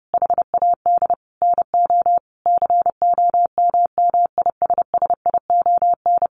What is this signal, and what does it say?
Morse code, heard as a keyed tone, about 30 wpm, 700 Hz. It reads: HAD NO COMMISSION
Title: Sentences from Top 1400 Words 30wpm